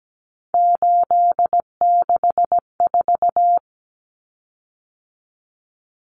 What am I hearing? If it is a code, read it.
Morse code, heard as a keyed tone, about 17 wpm, 700 Hz. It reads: 864
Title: Common Cut Numbers - 3-Digit 17wpm